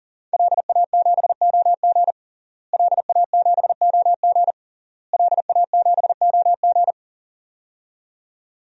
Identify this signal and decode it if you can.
Morse code, heard as a keyed tone, about 37 wpm, 700 Hz. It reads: LA7OG LA7OG LA7OG
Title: DXCC-100 Call Signs - 5 Characters - Repeated 3x - 40wpm